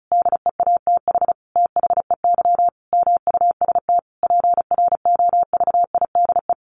Morse code, heard as a keyed tone, about 34 wpm, 700 Hz. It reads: DEATH THEY MUST PROVIDE
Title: Sentences from Top 1100 Words 35wpm